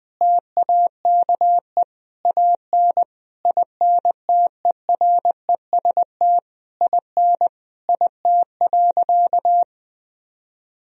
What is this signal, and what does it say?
Morse code, heard as a keyed tone, about 21 wpm, 700 Hz. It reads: TAKE AN INTEREST IN IT.
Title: Sentences from Top 1000 Words - Review Part 2 - 20wpm